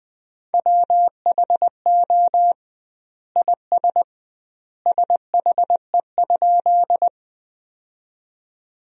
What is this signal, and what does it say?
Morse code, heard as a keyed tone, about 20 wpm, 700 Hz. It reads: WHO IS SHE?
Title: 2x Word-Spacing - Sentences from Top 100 Words - 20wpm